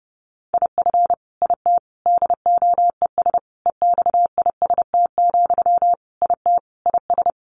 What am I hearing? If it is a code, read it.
Morse code, heard as a keyed tone, about 30 wpm, 700 Hz. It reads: IF IT DOES EXIST, IT IS
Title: Sentences from Top 1300 Words 30wpm